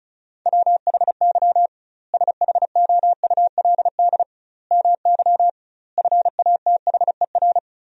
Morse code, heard as a keyed tone, about 36 wpm, 700 Hz. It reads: WHY SHOULD MY FATHER
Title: 2x Word-Spacing - Sentences from Top 500 Words - 35wpm